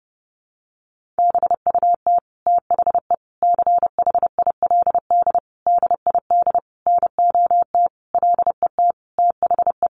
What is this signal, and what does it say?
Morse code, heard as a keyed tone, about 30 wpm, 700 Hz. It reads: BUT THE CHILD DID NOT LET THE
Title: Sentences from Top 1400 Words 30wpm